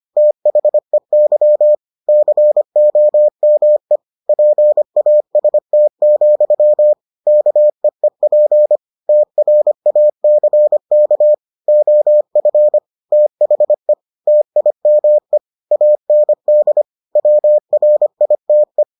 Morse code, heard as a keyed tone, about 25 wpm, 600 Hz. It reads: THEY COME PAST, KEEP TRACK OF THE TIME AND WRITE